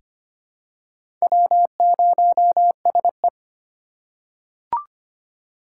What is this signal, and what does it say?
Morse code, heard as a keyed tone, about 25 wpm, 700 Hz. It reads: W0SE
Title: USA Call Signs - 4 Characters - 2x Character Spacing - 25wpm